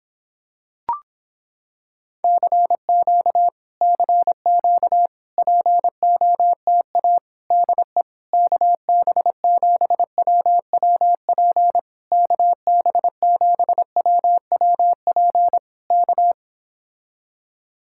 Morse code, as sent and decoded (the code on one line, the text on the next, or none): -.-. --.- / -.-. --.- / .--. --- - .- / -.. / -.- -... --... .-- .-- .--. / -.- -... --... .-- .-- .--. / -.-
CQ CQ POTA DE KB7WWP KB7WWP K